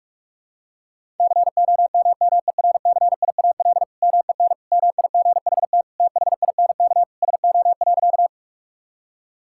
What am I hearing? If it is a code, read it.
Morse code, heard as a keyed tone, about 46 wpm, 690 Hz. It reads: COMMERCIAL MEN MIGHT THINK SO.
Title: Sentences from Top 1000 Words - Review Part 5 - 45wpm